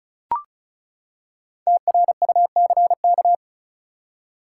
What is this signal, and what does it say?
Morse code, heard as a keyed tone, about 36 wpm, 700 Hz. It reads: TRUCK
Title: Five Letter Words - 35wpm